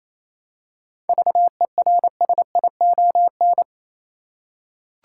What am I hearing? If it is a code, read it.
Morse code, heard as a keyed tone, about 28 wpm, 700 Hz. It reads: VERSION
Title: Seven Letter Words - 28wpm